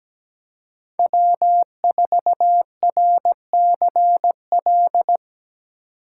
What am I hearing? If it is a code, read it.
Morse code, heard as a keyed tone, about 20 wpm, 700 Hz. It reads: W4RCL
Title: Speed Racing - USA 1x3 Call Signs 17wpm